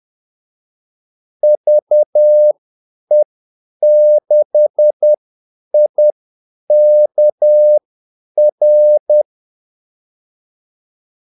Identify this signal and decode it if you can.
Morse code, heard as a keyed tone, about 10 wpm, spaced as for 8 wpm, 600 Hz. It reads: VE6IKR
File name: Callsigns-10-8wpm